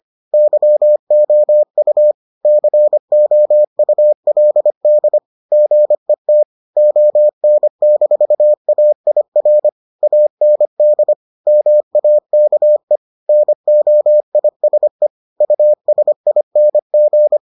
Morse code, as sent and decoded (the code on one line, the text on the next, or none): -.-- --- ..- / -.-. --- ..- .-.. -.. / --. - / --- -. -....- .- .. .-. / .- -. -.. / -- .- -.- . / -. --- .. ... / ..- ... .. -. --.
YOU COULD GET ON-AIR AND MAKE NOISE USING